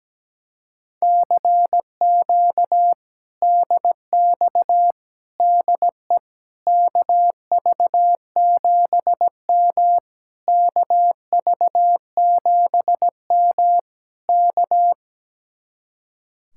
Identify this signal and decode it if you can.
Morse code, heard as a keyed tone, about 17 wpm, 700 Hz. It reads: CQ DX DE KV7M KV7M K
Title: Speed-Racing Common QSO Phrases - 17wpm